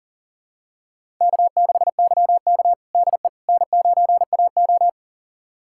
Morse code, heard as a keyed tone, about 40 wpm, 700 Hz. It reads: K6YK DE N9AO
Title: USA Callsign DE Callsign - 4 Characters - 40wpm